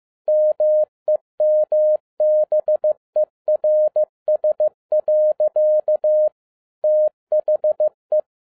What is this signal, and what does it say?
Morse code, heard as a keyed tone, about 15 wpm, 610 Hz. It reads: MEMBERS. THE
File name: 2011-10-mp3 / n12-JL3YMV 15wpm